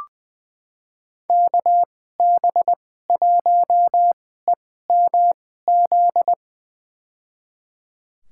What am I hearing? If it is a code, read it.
Morse code, heard as a keyed tone, about 20 wpm, spaced as for 15 wpm, 700 Hz. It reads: KB1EMZ
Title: USA Call Signs - 6 Characters - 2x Character Spacing - 20wpm